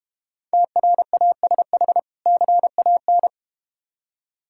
Morse code, heard as a keyed tone, about 30 wpm, 700 Hz. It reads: TRASH CAN
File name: Easy Words - Speed-Racing with Hints - 25wpm